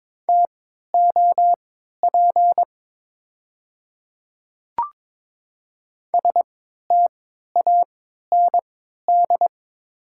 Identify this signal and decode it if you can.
Morse code, heard as a keyed tone, about 22 wpm, spaced as for 13 wpm, 700 Hz. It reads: TOP STAND
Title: 3x Character Spacing - Top 500 Words - 22wpm